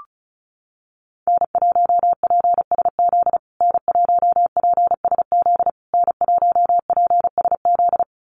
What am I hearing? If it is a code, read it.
Morse code, heard as a keyed tone, about 34 wpm, 700 Hz. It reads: N1PSZ N1PSZ N1PSZ
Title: USA Call Signs Repeated 3x 35wpm